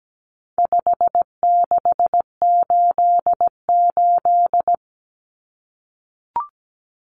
Morse code, heard as a keyed tone, about 17 wpm, 700 Hz. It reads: 5688
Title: Common Cut Numbers - 4-Digit 17wpm